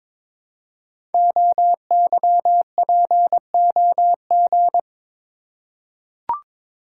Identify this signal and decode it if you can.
Morse code, heard as a keyed tone, about 22 wpm, 700 Hz. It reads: OYPOG
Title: Code Groups - 5 Letters - 22wpm